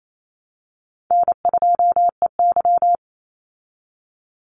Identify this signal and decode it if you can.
Morse code, heard as a keyed tone, about 28 wpm, 700 Hz. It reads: N2EY
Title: Easy Call Signs Repeated 3x - 28wpm